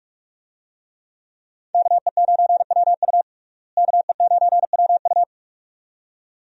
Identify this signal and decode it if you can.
Morse code, heard as a keyed tone, about 46 wpm, 700 Hz. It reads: KE9WU KE9WU
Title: USA Call Signs - 5 Characters - Repeated 2x - 45wpm